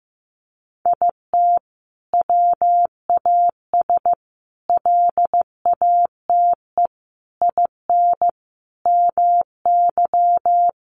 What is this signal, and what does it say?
Morse code, heard as a keyed tone, about 15 wpm, 700 Hz. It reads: IT WAS LATE IN MY
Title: Sentences from Top 1300 Words 15wpm